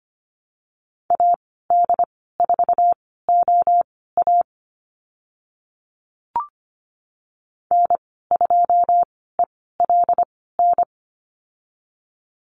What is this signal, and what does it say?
Morse code, heard as a keyed tone, about 25 wpm, spaced as for 16 wpm, 700 Hz. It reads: AD4OA N2ELN
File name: Lesson 15g - Number 5 - Call Signs 25wpm